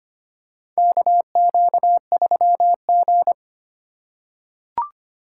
Morse code, heard as a keyed tone, about 25 wpm, 700 Hz. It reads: KQ3G
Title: USA Call Signs - 4 Characters - 2x Character Spacing - 25wpm